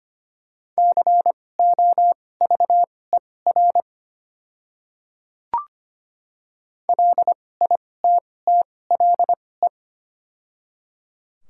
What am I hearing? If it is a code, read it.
Morse code, heard as a keyed tone, about 25 wpm, spaced as for 18 wpm, 690 Hz. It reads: COVER LITTLE
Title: 2x Character Spacing - Top 500 Words - 25wpm